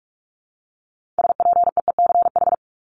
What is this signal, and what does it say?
Morse code, heard as a keyed tone, about 44 wpm, 700 Hz. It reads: SPEECH
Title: Top 1751-2000 Words 45wpm